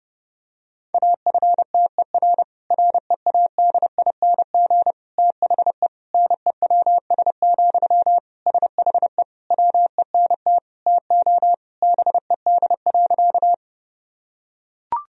Morse code, heard as a keyed tone, about 32 wpm, 700 Hz. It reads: AFTER READING THE NEWS, SHE WENT TO BED.
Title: Sentences from Top 1000 Words - Review Part 3 - 30wpm